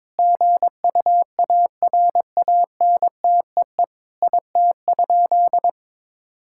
Morse code, heard as a keyed tone, about 21 wpm, 700 Hz. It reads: GUARANTEE IT?
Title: Sentences from Top 1600 Words 22wpm